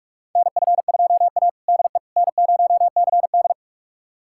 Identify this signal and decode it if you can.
Morse code, heard as a keyed tone, about 46 wpm, 690 Hz. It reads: NF2A DE N0CD
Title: USA Callsign DE Callsign - 4 Characters - 45wpm